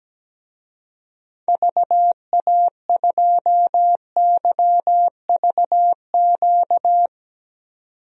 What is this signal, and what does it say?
Morse code, heard as a keyed tone, about 17 wpm, 700 Hz. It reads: VA2YVQ
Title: DXCC-100 Call Signs - 6 Characters - 17wpm